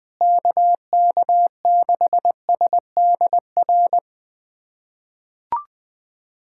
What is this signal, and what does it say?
Morse code, heard as a keyed tone, about 20 wpm, 700 Hz. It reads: KK6SDR
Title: USA Callsign DE Callsign - 6 Characters - 20wpm